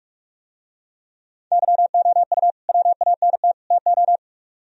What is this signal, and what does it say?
Morse code, heard as a keyed tone, about 45 wpm, 700 Hz. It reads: YOU WANT TO